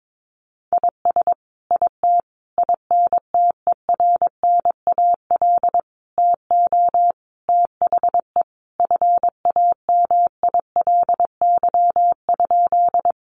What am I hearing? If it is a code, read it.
Morse code, heard as a keyed tone, about 22 wpm, 700 Hz. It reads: IS IT INTERNAL TO THE FAMILY?
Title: Sentences from Top 1400 Words 22wpm